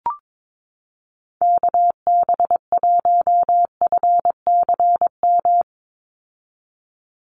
Morse code, heard as a keyed tone, about 22 wpm, 700 Hz. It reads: KB1FCM
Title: USA Call Signs 22wpm